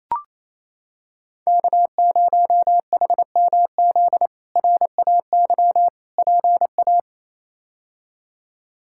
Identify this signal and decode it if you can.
Morse code, heard as a keyed tone, about 28 wpm, 700 Hz. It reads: K0HMZ RAY PA
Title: K1USN SST QSO Practice - 28wpm